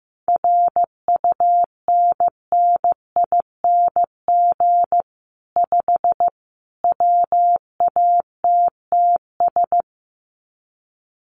Morse code, heard as a keyed tone, about 15 wpm, 700 Hz. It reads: RUNNING 5 WATTS
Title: Common QSO Phrases 15wpm